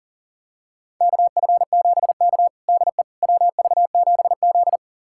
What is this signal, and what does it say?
Morse code, heard as a keyed tone, about 40 wpm, 700 Hz. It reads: KF7K DE WV7Z
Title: USA Callsign DE Callsign - 4 Characters - 40wpm